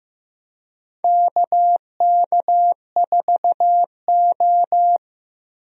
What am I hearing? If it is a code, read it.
Morse code, heard as a keyed tone, about 15 wpm, 700 Hz. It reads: KK4O
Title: USA Call Signs - 4 Characters - Repeated 2x - 15wpm